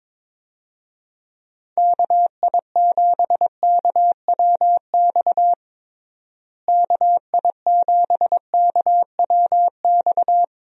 Morse code, heard as a keyed tone, about 22 wpm, 700 Hz. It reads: KI7KWX KI7KWX
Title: USA Call Signs - 6 Characters - Repeated 3x - 22wpm